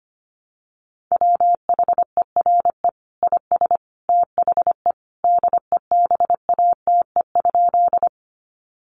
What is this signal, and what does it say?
Morse code, heard as a keyed tone, about 25 wpm, 700 Hz. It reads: WHERE IS THE DEBATE?
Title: Sentences from Top 1100 Words 25wpm